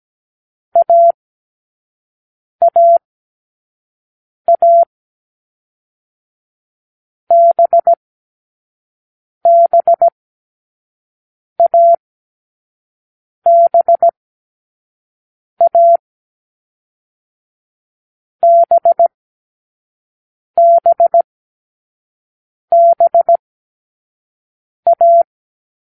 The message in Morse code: A A A B B A B A B B B A